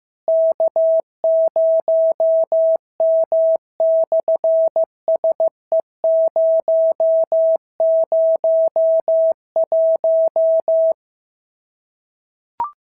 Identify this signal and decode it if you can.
Morse code, heard as a keyed tone, about 15 wpm, 650 Hz. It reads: K0M/SE001
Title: SOTA QSO Practice 15wpm